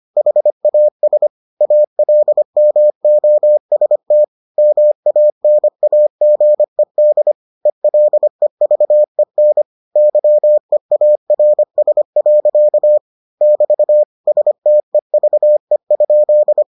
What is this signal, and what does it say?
Morse code, heard as a keyed tone, about 25 wpm, 600 Hz. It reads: HAS ALMOST MANAGED ELEVEN YEARS. <BT> STEVE?